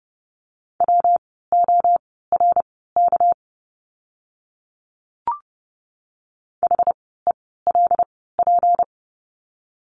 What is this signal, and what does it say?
Morse code, heard as a keyed tone, about 30 wpm, spaced as for 17 wpm, 700 Hz. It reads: WORK HELP
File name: course / 30:10wpm / 237 Lesson 38f - Letter q - Words 30wpm